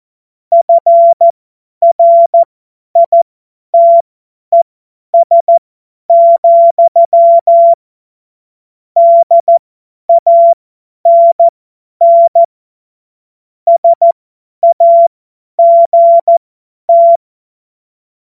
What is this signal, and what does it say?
Morse code, heard as a keyed tone, about 14 wpm, spaced as for 10 wpm, 680 Hz. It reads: FRITES, DANN SAGT
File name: Morsekurs / Witze / Witz 10.Oktober 2023 50 F70